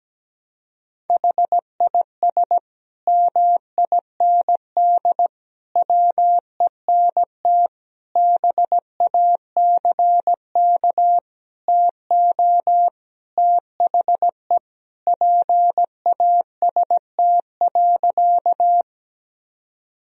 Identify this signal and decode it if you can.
Morse code, heard as a keyed tone, about 17 wpm, 700 Hz. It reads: HIS MIND WENT BACK TO THE PAST.
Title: Sentences from Top 1000 Words - Review Part 3 - 17wpm